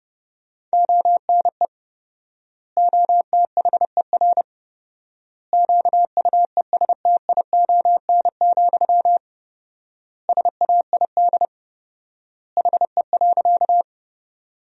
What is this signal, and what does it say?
Morse code, heard as a keyed tone, about 32 wpm, 700 Hz. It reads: ONE OTHER QUESTION, SAID HE.
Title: 4x Word-Spacing - Sentences from Top 400 Words - 30wpm